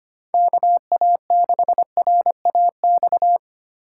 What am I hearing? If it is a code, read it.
Morse code, heard as a keyed tone, about 29 wpm, 700 Hz. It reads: KA6RAX
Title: Speed Racing - USA 2x3 Call Signs 25wpm